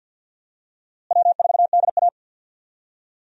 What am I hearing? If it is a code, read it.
Morse code, heard as a keyed tone, about 53 wpm, 700 Hz. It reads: W4DA
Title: USA Call Signs - 4 Characters - Repeated 2x - 50wpm